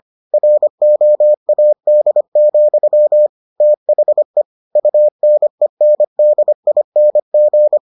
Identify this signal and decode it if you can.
Morse code, heard as a keyed tone, about 25 wpm, 600 Hz. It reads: ROAD, THE UNENDING